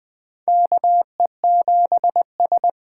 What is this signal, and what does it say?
Morse code, heard as a keyed tone, about 20 wpm, 700 Hz. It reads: KE7S
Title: USA Call Signs - 6 Characters - 8x Character Spacing - 20wpm